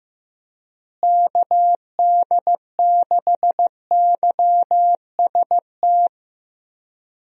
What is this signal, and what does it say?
Morse code, heard as a keyed tone, about 15 wpm, 700 Hz. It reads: KD6YST
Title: USA Call Signs - 6 Characters - Repeated 3x - 15wpm